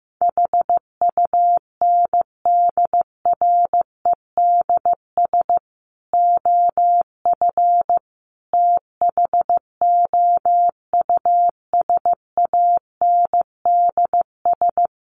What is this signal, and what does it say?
Morse code, heard as a keyed tone, about 15 wpm, 700 Hz. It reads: HUNDREDS OF THOUSANDS
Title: Sets of 5 Words Encore2 15wpm